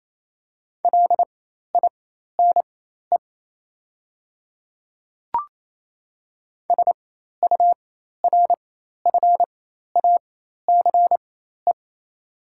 Morse code, heard as a keyed tone, about 28 wpm, spaced as for 13 wpm, 690 Hz. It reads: LINE SURFACE
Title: 4x Character Spacing - Top 500 Words - 28wpm